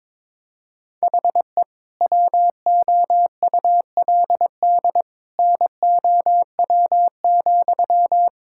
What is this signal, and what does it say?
Morse code, heard as a keyed tone, about 22 wpm, 700 Hz. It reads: HE WOULD NOW,
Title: Sentences from Top 1000 Words - Review Part 5 - 22wpm